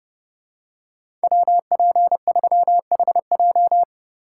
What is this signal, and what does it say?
Morse code, heard as a keyed tone, about 29 wpm, 700 Hz. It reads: WP3HJ